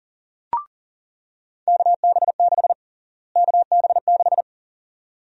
Morse code, heard as a keyed tone, about 40 wpm, 700 Hz. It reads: KB6 KB6
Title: USA Call Sign Prefixes - Repeated 2x - 40wpm